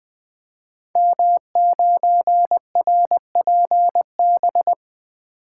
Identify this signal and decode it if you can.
Morse code, heard as a keyed tone, about 20 wpm, 700 Hz. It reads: M9RPB
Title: Code Groups - 5 Letter:Numbers - 20wpm